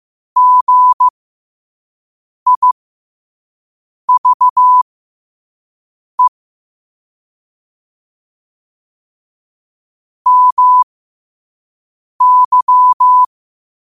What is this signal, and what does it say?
Morse code, heard as a keyed tone, about 15 wpm, spaced as for 5 wpm, 1000 Hz. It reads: GIVE MY
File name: qotdfiles / 5wpm